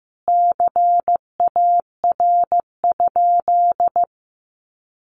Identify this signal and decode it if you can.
Morse code, heard as a keyed tone, about 15 wpm, 700 Hz. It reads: CAR?
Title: Sentences from Top 1200 Words 15wpm